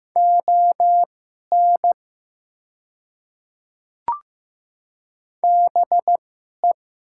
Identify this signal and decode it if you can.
Morse code, heard as a keyed tone, about 15 wpm, spaced as for 11 wpm, 700 Hz. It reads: ON BE
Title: 2x Character Spacing - Top 100 Words - 15wpm